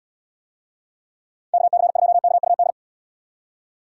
Morse code, heard as a keyed tone, about 78 wpm, 700 Hz. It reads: KC2KUD